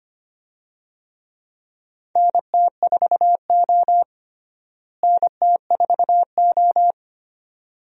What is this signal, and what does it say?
Morse code, heard as a keyed tone, about 25 wpm, 700 Hz. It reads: NT4O NT4O
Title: USA Call Signs - 4 Characters - Repeated 2x - 25wpm